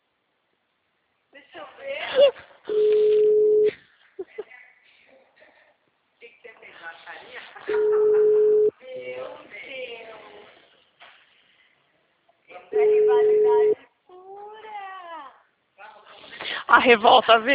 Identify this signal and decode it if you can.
Ringback tone (Europe, 425 Hz), with speech.
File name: out- -2033- - - .41931